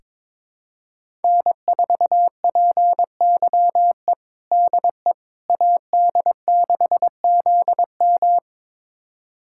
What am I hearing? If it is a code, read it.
Morse code, heard as a keyed tone, about 22 wpm, 700 Hz. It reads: N4PYE DE AD6ZM
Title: USA Callsign DE Callsign - 5 Characters - 22wpm